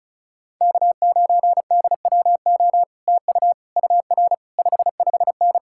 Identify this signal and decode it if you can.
Morse code, heard as a keyed tone, about 37 wpm, 700 Hz. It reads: K9DWO TU UR 55N
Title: Speed-Racing Common QSO Phrases Encore - 30wpm